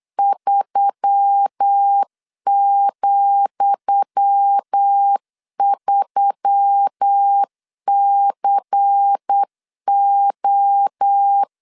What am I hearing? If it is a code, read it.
Morse code, heard as a keyed tone, about 8 wpm, 800 Hz. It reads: 3,3CO
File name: LEZIONI / 10 SIGNS / 10D COMA 8wpm 3749